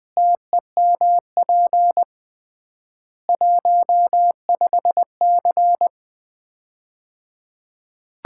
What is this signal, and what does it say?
Morse code, heard as a keyed tone, about 21 wpm, 700 Hz. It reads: TEMP 15C
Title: Common QSO Phrases - 3x Word-Spacing - 20wpm